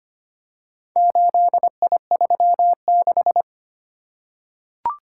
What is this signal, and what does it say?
Morse code, heard as a keyed tone, about 25 wpm, 690 Hz. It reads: 8I36
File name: practice / 25wpm / Four Letter-Number Combinations - 25wpm